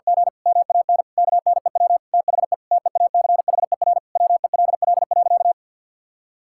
Morse code, heard as a keyed tone, about 50 wpm, 700 Hz. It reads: G MAN KNEW THE TEACHER WELL.
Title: Sentences from Top 1000 Words - Review Part 6 - 50wpm